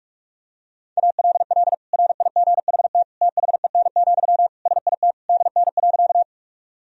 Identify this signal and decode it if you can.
Morse code, heard as a keyed tone, about 46 wpm, 700 Hz. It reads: ALL RIGHT THEN, SIT DN.
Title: Sentences from Top 500 Words and Abbreviations 45wpm